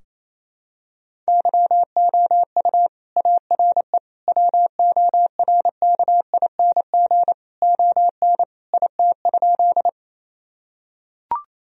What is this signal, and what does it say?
Morse code, heard as a keyed tone, about 28 wpm, 700 Hz. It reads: YOU ARE WORKING ON IT?
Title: Sentences from Top 1000 Words - Review Part 4 - 28wpm